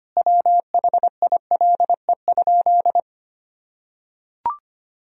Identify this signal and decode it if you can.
Morse code, heard as a keyed tone, about 25 wpm, 700 Hz. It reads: WHILE?
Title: Sentences from Top 300 Words and Abbreviations 25wpm